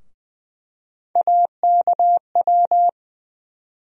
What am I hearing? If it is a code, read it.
Morse code, heard as a keyed tone, about 20 wpm, 700 Hz. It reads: AKW
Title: USA Call Sign Postfixes - 3 Letters 20wpm